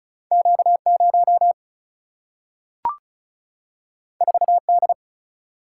Morse code, heard as a keyed tone, about 36 wpm, 700 Hz. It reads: Q 0 4 D
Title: Single-Letter Single-Number 35wpm